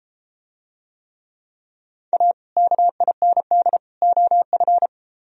Morse code, heard as a keyed tone, about 32 wpm, 700 Hz. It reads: A KIND OF